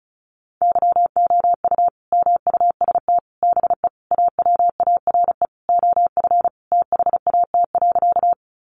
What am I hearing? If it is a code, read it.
Morse code, heard as a keyed tone, about 34 wpm, 700 Hz. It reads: YOU MUST BE AWARE OF THAT.
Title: Sentences from Top 1200 Words 35wpm